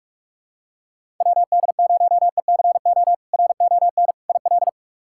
Morse code, heard as a keyed tone, about 42 wpm, 700 Hz. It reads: WD0ECO RON IL
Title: K1USN SST QSO Practice - 45wpm